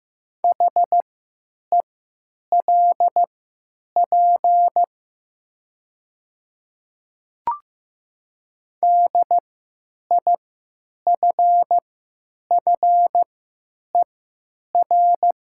Morse code, heard as a keyed tone, about 15 wpm, spaced as for 9 wpm, 700 Hz. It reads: HELP DIFFER
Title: 3x Character Spacing - Top 200 Words - 15wpm